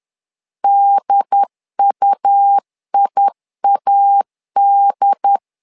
Morse code, heard as a keyed tone, about 10 wpm, 800 Hz. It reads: DUIAD
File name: LEZIONI / 2 UHDN / 2 UHD 10wpm 1826